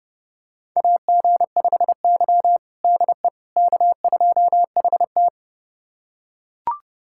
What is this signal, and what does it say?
Morse code, heard as a keyed tone, about 31 wpm, 700 Hz. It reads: AG5Y DE K2HT
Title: USA Callsign DE Callsign - 4 Characters - 30wpm